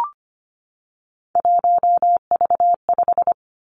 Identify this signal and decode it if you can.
Morse code, heard as a keyed tone, about 25 wpm, 700 Hz. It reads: 1V5
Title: Three Challenging Letter-Number Combinations 25wpm